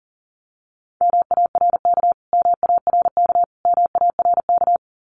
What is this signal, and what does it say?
Morse code, heard as a keyed tone, about 40 wpm, 700 Hz. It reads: MARK MARK MARK
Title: Top 500 Words Repeated 40wpm